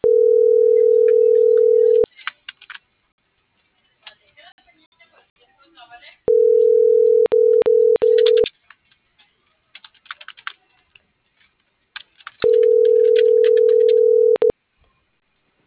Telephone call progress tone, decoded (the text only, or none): ringback tone